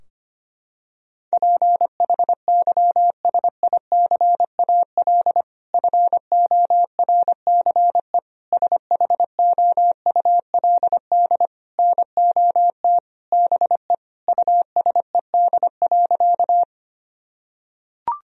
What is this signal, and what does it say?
Morse code, heard as a keyed tone, about 25 wpm, 700 Hz. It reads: PHYSICAL FORCE SHOULD NOT BE USED.
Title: Sentences from Top 1000 Words - Review Part 4 - 25wpm